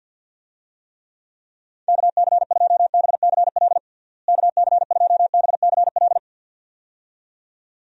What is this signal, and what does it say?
Morse code, heard as a keyed tone, about 46 wpm, 700 Hz. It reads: KC2BCL KC2BCL
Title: USA Call Signs - 6 Characters - Repeated 2x - 50wpm